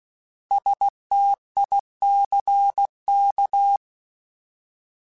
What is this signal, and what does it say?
Morse code, heard as a keyed tone, about 16 wpm, 800 Hz. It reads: STICK